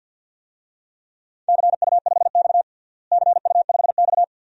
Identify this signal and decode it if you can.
Morse code, heard as a keyed tone, about 52 wpm, 700 Hz. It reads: CU5X CU5X
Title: DXCC-100 Call Signs - 4 Characters - Repeated 2x - 50wpm